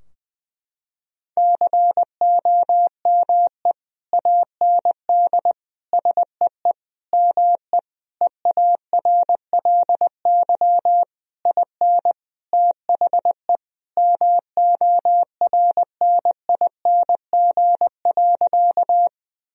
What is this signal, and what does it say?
Morse code, heard as a keyed tone, about 20 wpm, 700 Hz. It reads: COME AND SEE ME EARLY IN THE MORNING.
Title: Sentences from Top 1000 Words - Review Part 2 - 20wpm